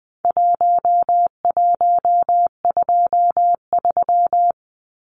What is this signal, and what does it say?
Morse code, heard as a keyed tone, about 20 wpm, 700 Hz. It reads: 1123
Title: Numbers - 4-Digit 20wpm